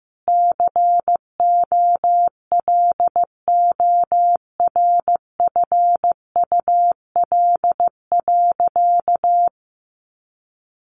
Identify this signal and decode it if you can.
Morse code, heard as a keyed tone, about 15 wpm, 700 Hz. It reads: COLORFUL.
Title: Easy Phrases 15wpm